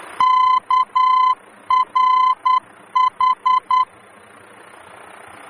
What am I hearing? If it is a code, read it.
Morse code, heard as a keyed tone, about 10 wpm, 1020 Hz. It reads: KRH
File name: germany / mx / KRH-D-VOR